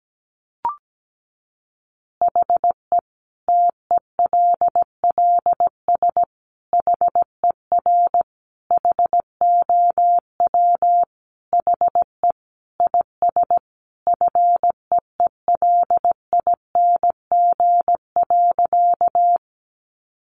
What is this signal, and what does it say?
Morse code, heard as a keyed tone, about 17 wpm, 700 Hz. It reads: HE TELLS HER HOW HE IS FEELING.
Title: Additional Easy Phrases 17wpm